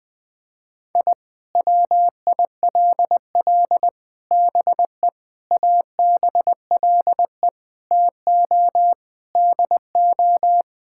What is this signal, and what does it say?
Morse code, heard as a keyed tone, about 20 wpm, 700 Hz. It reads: I WILL BE ABLE TO DO
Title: Sentences with 4-Word Phrases - Spoken Beginning - 20wpm